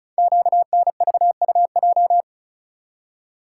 Morse code, heard as a keyed tone, about 36 wpm, 700 Hz. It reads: QNVUJ
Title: Code Groups - 5 Letters - 35wpm